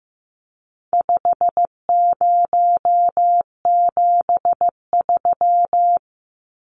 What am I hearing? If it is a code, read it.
Morse code, heard as a keyed tone, about 15 wpm, 700 Hz. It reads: 5073
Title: Numbers - 4-Digit 15wpm